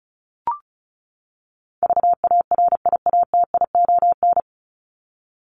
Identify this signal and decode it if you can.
Morse code, heard as a keyed tone, about 34 wpm, 700 Hz. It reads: VARIATION
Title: Top 1501-1750 Words 35wpm